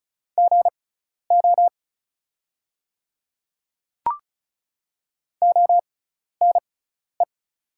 Morse code, heard as a keyed tone, about 36 wpm, spaced as for 12 wpm, 700 Hz. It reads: GO ONE
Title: 6x Character Spacing - Top 100 Words - 35wpm